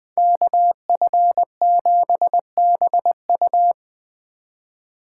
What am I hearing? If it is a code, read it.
Morse code, heard as a keyed tone, about 21 wpm, 700 Hz. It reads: KF7BU